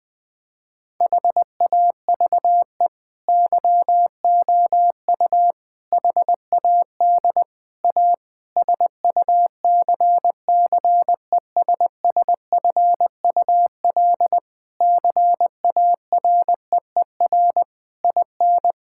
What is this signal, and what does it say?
Morse code, heard as a keyed tone, about 20 wpm, 700 Hz. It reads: HAVE YOU HAD A SUCCESSFUL CAREER IN
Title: Sentences from Top 1000 Words - Review Part 6 - 20wpm